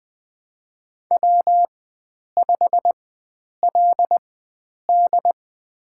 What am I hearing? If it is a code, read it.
Morse code, heard as a keyed tone, about 20 wpm, spaced as for 9 wpm, 700 Hz. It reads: W5LD
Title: USA Call Signs - 4 Characters - 4x Character Spacing - 20wpm